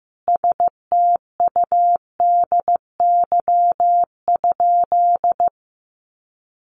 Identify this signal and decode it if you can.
Morse code, heard as a keyed tone, about 15 wpm, 700 Hz. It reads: STUDY?
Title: Sentences from Top 1400 Words 15wpm